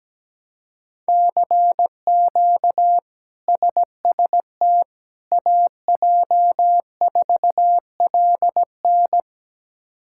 Morse code, heard as a keyed tone, about 17 wpm, 700 Hz. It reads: CQ SST AJ4LN
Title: K1USN SST QSO Practice - 17wpm